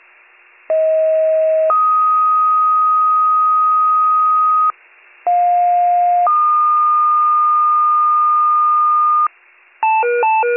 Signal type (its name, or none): two-tone page